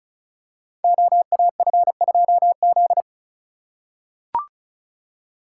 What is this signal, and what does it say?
Morse code, heard as a keyed tone, about 36 wpm, 700 Hz. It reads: OAF2Z
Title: Code Groups - 5 Letter:Numbers - 35wpm